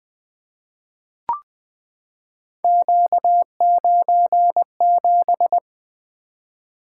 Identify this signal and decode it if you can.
Morse code, heard as a keyed tone, about 20 wpm, 700 Hz. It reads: Q97
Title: Three Letter-Number Combinations - 20wpm